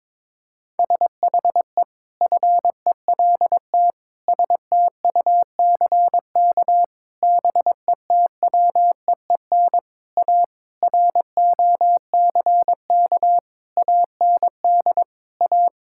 Morse code, heard as a keyed tone, about 22 wpm, 700 Hz. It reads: SHE FELT STUCK BETWEEN A ROCK AND A
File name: practice / 22wpm / Sentences with 8-Word Phrases - 22wpm